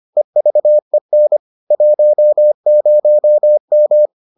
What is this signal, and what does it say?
Morse code, heard as a keyed tone, about 25 wpm, 600 Hz. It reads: EVEN 10M